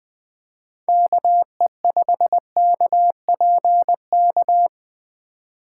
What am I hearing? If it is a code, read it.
Morse code, heard as a keyed tone, about 20 wpm, 700 Hz. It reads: KE5KPK
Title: Speed Racing - USA Call Signs 17wpm